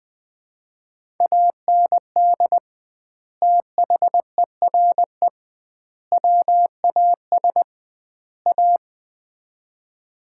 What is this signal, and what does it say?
Morse code, heard as a keyed tone, about 20 wpm, 700 Hz. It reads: AND THERE WAS A
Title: 2x Word-Spacing - Sets of 4 Words - 20wpm